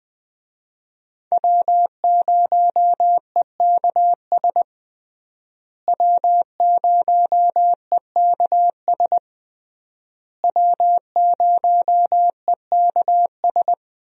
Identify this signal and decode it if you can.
Morse code, heard as a keyed tone, about 21 wpm, 700 Hz. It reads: W0EKS W0EKS W0EKS
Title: USA Call Signs - 5 Characters - Repeated 3x - 20wpm